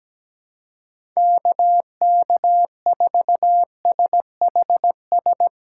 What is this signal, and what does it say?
Morse code, heard as a keyed tone, about 17 wpm, 700 Hz. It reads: KK4SHS
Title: USA Call Signs - 6 Characters - 2x Character Spacing - 17wpm